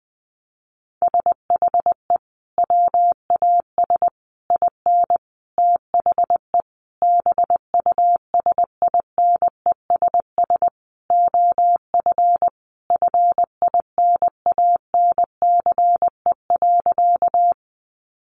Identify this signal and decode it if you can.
Morse code, heard as a keyed tone, about 20 wpm, 700 Hz. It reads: SHE WAS IN THE BUSINESS OF FINANCE.
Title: Sentences from Top 1500 Words 20wpm